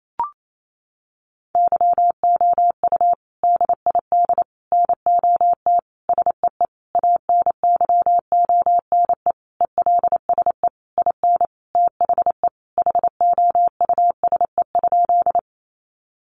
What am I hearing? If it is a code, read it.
Morse code, heard as a keyed tone, about 28 wpm, 700 Hz. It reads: YOU DID NOT SEE ANYONE ELSE IN THE HOUSE?
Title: Sentences from Top 900 Words 28wpm